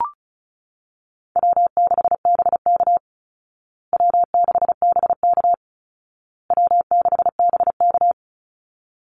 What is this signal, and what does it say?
Morse code, heard as a keyed tone, about 34 wpm, 700 Hz. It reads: W6BK W6BK W6BK
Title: Easy Call Signs Repeated 3x - 35wpm